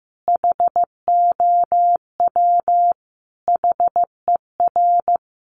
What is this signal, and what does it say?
Morse code, heard as a keyed tone, about 14 wpm, 700 Hz. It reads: HOW HER
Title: Sentences from Top 700 Words 15wpm